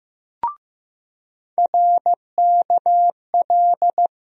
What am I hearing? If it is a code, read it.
Morse code, heard as a keyed tone, about 15 wpm, 700 Hz. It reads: RKL
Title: USA Call Sign Postfixes - 3 Letters 15wpm